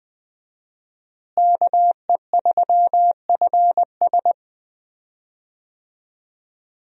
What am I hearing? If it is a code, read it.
Morse code, heard as a keyed tone, about 20 wpm, 700 Hz. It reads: KE3FS
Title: USA Call Signs - 5 Characters - Repeated 2x - 20wpm